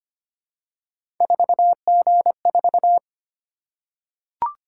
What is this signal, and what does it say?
Morse code, heard as a keyed tone, about 25 wpm, 700 Hz. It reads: 4G4
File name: Three Letter-Number Combinations - 25wpm